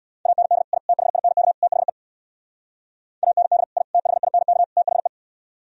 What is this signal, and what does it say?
Morse code, heard as a keyed tone, about 59 wpm, 700 Hz. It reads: AND I THINK THE AND I THINK THE
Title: Speed-Racing - Sets of 4 Words 50wpm